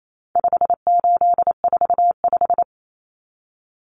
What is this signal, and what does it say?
Morse code, heard as a keyed tone, about 28 wpm, 700 Hz. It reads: 5845
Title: Common Cut Numbers - 4-Digit 28wpm